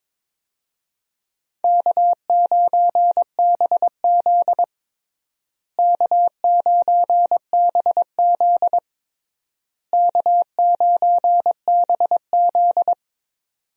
Morse code, heard as a keyed tone, about 21 wpm, 700 Hz. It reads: K9BZ K9BZ K9BZ
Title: USA Call Signs - 4 Characters - Repeated 3x - 22wpm